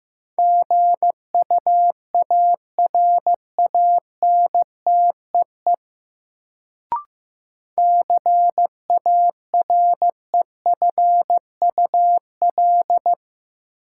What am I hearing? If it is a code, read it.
Morse code, heard as a keyed tone, about 15 wpm, 700 Hz. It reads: GUARANTEE CAREFUL
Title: Sentences from Top 1600 Words - New Word Drill - 15wpm